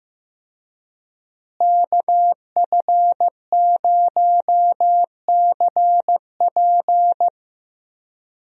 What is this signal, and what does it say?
Morse code, heard as a keyed tone, about 15 wpm, 700 Hz. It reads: KF0CP
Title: USA Call Signs - 5 Characters - Repeated 3x - 15wpm